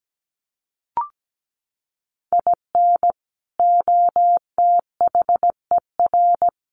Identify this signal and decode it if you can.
Morse code, heard as a keyed tone, about 17 wpm, 700 Hz. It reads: IN OTHER
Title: Sets of 3 Words 17wpm